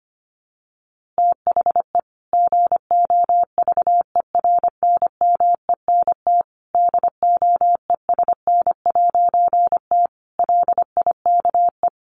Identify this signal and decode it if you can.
Morse code, heard as a keyed tone, about 25 wpm, 700 Hz. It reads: THE GOVERNMENT DOESN'T LIKE
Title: A Sound Of Thunder 25wpm